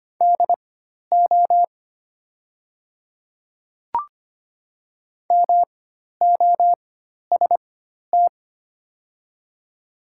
Morse code, heard as a keyed tone, about 25 wpm, spaced as for 12 wpm, 700 Hz. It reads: DO MOST
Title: 4x Character Spacing - Top 100 Words - 25wpm